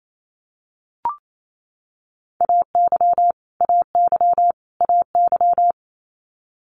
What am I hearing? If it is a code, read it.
Morse code, heard as a keyed tone, about 28 wpm, 700 Hz. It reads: AY AY AY
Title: Multi-letter Phonemes Repeated 3x 28wpm